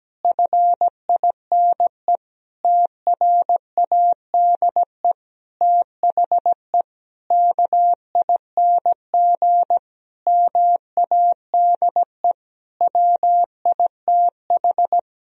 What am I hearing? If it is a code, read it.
Morse code, heard as a keyed tone, about 17 wpm, 700 Hz. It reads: FINE TRADE THE KING MADE WITH
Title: Sentences from Top 1000 Words - Review Part 4 - 17wpm